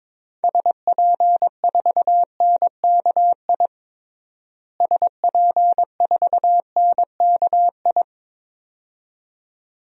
Morse code, heard as a keyed tone, about 21 wpm, 700 Hz. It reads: SP4NKI SP4NKI
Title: DXCC-100 Call Signs - 6 Characters - Repeated 2x - 22wpm